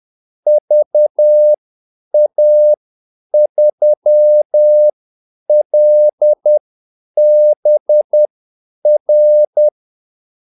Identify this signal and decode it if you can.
Morse code, heard as a keyed tone, about 10 wpm, spaced as for 8 wpm, 600 Hz. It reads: VA3LBR
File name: Callsigns-10-8wpm